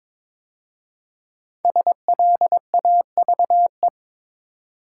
Morse code, heard as a keyed tone, about 23 wpm, 700 Hz. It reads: SLAVE
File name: 5 Letter Words - First 3 Letters Given - 22wpm